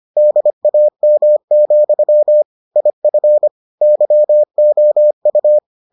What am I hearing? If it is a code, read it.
Morse code, heard as a keyed tone, about 25 wpm, 600 Hz. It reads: DAM, IF YOU